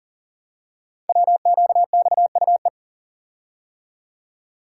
Morse code, heard as a keyed tone, about 40 wpm, 700 Hz. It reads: WQXUE
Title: Code Groups - 5 Letters - 40wpm